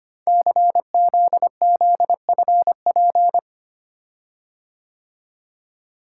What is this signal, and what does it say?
Morse code, heard as a keyed tone, about 25 wpm, 700 Hz. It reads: CZZFP
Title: Code Groups - 5 Letter:Numbers - 25wpm